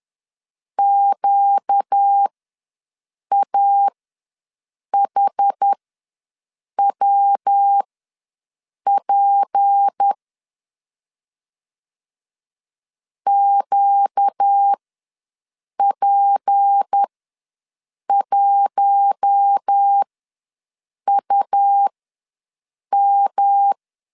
Morse code, heard as a keyed tone, about 10 wpm, spaced as for 6 wpm, 800 Hz. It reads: QAHWP QP1UM
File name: LEZIONI / 6 PKX0 / 6 P ws 10wpm 3751